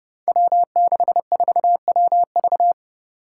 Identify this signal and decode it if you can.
Morse code, heard as a keyed tone, about 29 wpm, 700 Hz. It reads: W64WV
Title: Five Challenging Letter-Number Combinations 30wpm